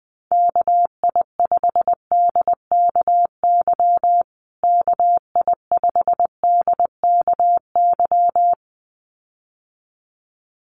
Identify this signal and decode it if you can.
Morse code, heard as a keyed tone, about 20 wpm, 700 Hz. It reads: KI5DKY KI5DKY
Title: USA Call Signs Repeated 2x 20wpm